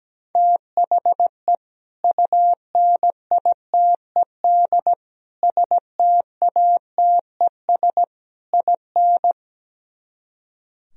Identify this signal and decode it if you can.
Morse code, heard as a keyed tone, about 17 wpm, 700 Hz. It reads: THE UNITED STATES IN
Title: Speed-Racing - Sets of 4 Words 17wpm